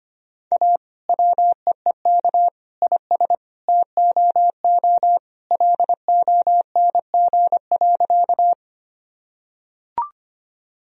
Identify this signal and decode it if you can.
Morse code, heard as a keyed tone, about 25 wpm, 700 Hz. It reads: A WEEK IS TOO LONG.
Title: Sentences from Top 1000 Words - Review Part 2 - 25wpm